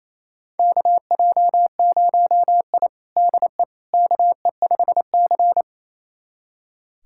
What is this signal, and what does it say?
Morse code, heard as a keyed tone, about 28 wpm, 700 Hz. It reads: KJ0I DE KE5C
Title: USA Callsign DE Callsign - 4 Characters - 28wpm